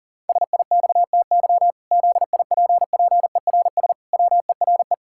Morse code, heard as a keyed tone, about 40 wpm, 700 Hz. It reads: SIXTY ZIPPERS WERE
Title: Pangram Challenge v2 - 40wpm